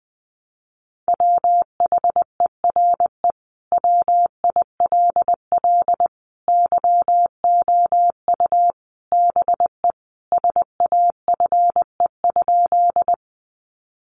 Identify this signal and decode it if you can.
Morse code, heard as a keyed tone, about 20 wpm, 700 Hz. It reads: WHERE WILL YOU BE SAFE?
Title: Sentences from Top 900 Words 20wpm